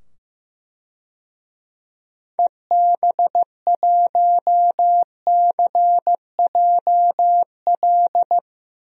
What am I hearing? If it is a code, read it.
Morse code, heard as a keyed tone, about 15 wpm, 700 Hz. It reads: EB1CJL
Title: DXCC-100 Call Signs - 6 Characters - 15wpm